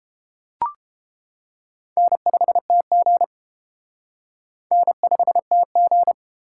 Morse code, heard as a keyed tone, about 32 wpm, 700 Hz. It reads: N5TG N5TG